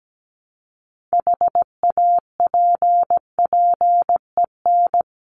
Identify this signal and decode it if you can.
Morse code, heard as a keyed tone, about 17 wpm, 700 Hz. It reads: HAPPEN
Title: ED Suffix - Top 100 Words with Base Words 17wpm